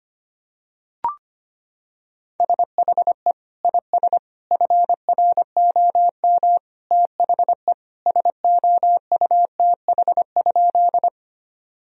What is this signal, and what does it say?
Morse code, heard as a keyed tone, about 25 wpm, 700 Hz. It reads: SHE IS FROM THE SOUTH?
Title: Sentences from Top 1000 Words - Review Part 2 - 25wpm